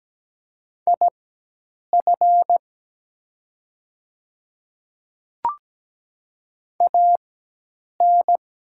Morse code, heard as a keyed tone, about 17 wpm, spaced as for 8 wpm, 700 Hz. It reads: IF AN
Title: 4x Character Spacing - Top 100 Words - 17wpm